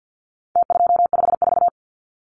Morse code, heard as a keyed tone, about 50 wpm, 700 Hz. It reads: T254
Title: Numbers - Contest Style - 4-Digit 50wpm